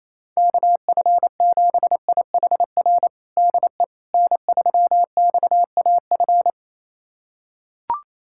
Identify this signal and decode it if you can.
Morse code, heard as a keyed tone, about 28 wpm, 700 Hz. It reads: KF7IHR DE N3XAF
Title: USA Callsign DE Callsign 28wpm